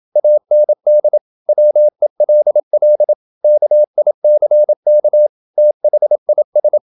Morse code, heard as a keyed tone, about 27 wpm, 600 Hz. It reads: AND WELL KICK THIS
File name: Never 0027wpm